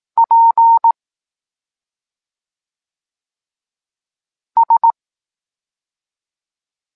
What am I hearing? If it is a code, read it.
Morse code, heard as a keyed tone, about 18 wpm, 920 Hz. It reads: P S
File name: Tekst-26 1